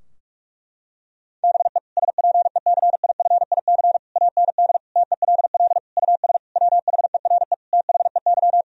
Morse code, heard as a keyed tone, about 41 wpm, 700 Hz. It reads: BE SPECIFIC AND TELL US WHERE THEY
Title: Sentences from Top 1000 Words - Review Part 3 - 45wpm